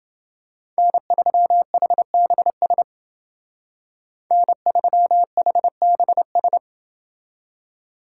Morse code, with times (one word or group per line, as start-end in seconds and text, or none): 0.8-2.8 N3HBS
4.3-6.6 N3HBS